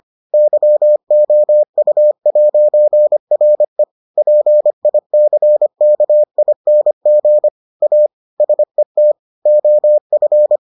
Morse code, heard as a keyed tone, about 25 wpm, 600 Hz. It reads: YOU'RE PICKING A SET OF